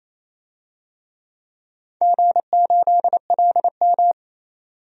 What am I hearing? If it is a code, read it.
Morse code, heard as a keyed tone, about 28 wpm, 700 Hz. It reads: G8LM
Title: DXCC-100 Call Signs - 4 Characters - 28wpm